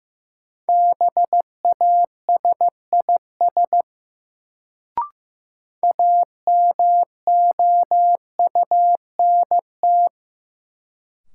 Morse code, heard as a keyed tone, about 15 wpm, 700 Hz. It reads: BASIS AMOUNT
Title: Sets of 3 Words - New Word Drill - 15wpm